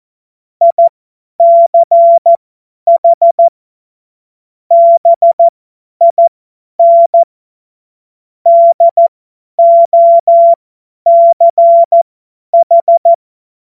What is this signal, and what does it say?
Morse code, heard as a keyed tone, about 14 wpm, spaced as for 10 wpm, 680 Hz. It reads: ICH BIN DOCH
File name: Morsekurs / Witze / Witz 09 April 2024 50 F70